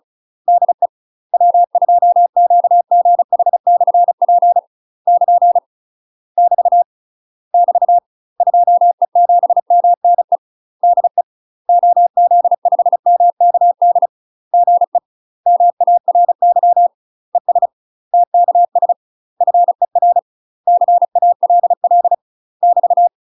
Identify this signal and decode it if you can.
Morse code, heard as a keyed tone, about 36 wpm, 700 Hz. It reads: DE W2QGH/P <KN> <BT> <BT> 2E7MNE DE OZ5MKD GE MARY ES TKS FER CALL <BT>
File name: UK MASTER QSOs TRAINING / 35WPM / UK MASTER QSOs 0009